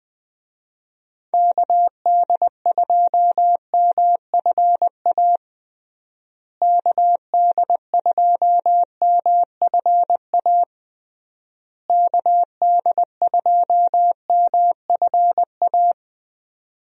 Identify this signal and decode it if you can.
Morse code, heard as a keyed tone, about 21 wpm, 700 Hz. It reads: KD2MFA KD2MFA KD2MFA
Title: USA Call Signs - 6 Characters - Repeated 3x - 20wpm